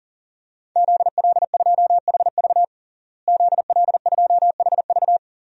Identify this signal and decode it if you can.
Morse code, heard as a keyed tone, about 40 wpm, 700 Hz. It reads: ZL2HV ZL2HV
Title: DXCC-100 Call Signs - 5 Characters - Repeated 3x - 40wpm